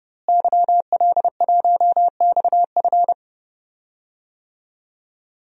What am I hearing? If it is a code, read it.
Morse code, heard as a keyed tone, about 31 wpm, 700 Hz. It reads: YL1XF
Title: DXCC-100 Call Signs - 5 Characters - Repeated 3x - 30wpm